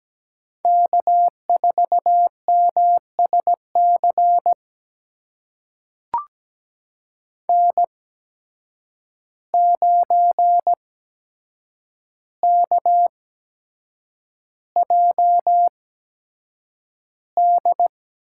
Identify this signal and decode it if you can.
Morse code, heard as a keyed tone, about 17 wpm, spaced as for 7 wpm, 700 Hz. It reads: K4MSC N9KJD
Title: USA Call Signs - 5 Characters - 8x Character Spacing - 17wpm